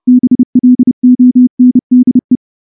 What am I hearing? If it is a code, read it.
Morse code, heard as a keyed tone, about 28 wpm, 250 Hz. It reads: BLONDE
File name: morse-training-30WPM